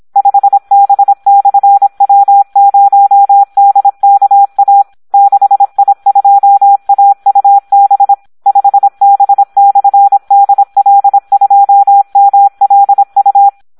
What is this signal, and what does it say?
Morse code, heard as a keyed tone, about 26 wpm, 800 Hz. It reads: 5B/W0DKA 6I2AUB 5B/DL2MLU
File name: Audio practica CW / SD 1 26